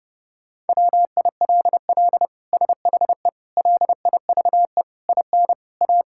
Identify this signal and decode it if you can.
Morse code, heard as a keyed tone, about 30 wpm, 700 Hz. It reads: WILL SHE LIVE IN A
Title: Sentences from Top 1000 Words - Review Part 6 - 30wpm